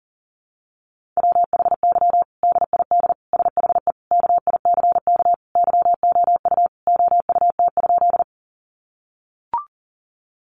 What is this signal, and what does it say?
Morse code, heard as a keyed tone, about 37 wpm, 700 Hz. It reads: WHY DID SHE KICK YOU OUT?
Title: Sentences from Top 1300 Words 40wpm